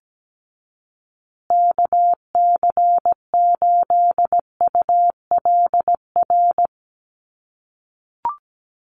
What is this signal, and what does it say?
Morse code, heard as a keyed tone, about 17 wpm, 700 Hz. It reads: KC8ULR
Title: USA Call Signs Repeated 2x 17wpm